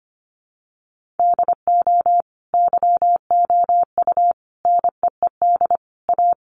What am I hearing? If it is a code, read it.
Morse code, heard as a keyed tone, about 25 wpm, 700 Hz. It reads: DO YOU NEED A
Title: Sentences from Top 1500 Words 25wpm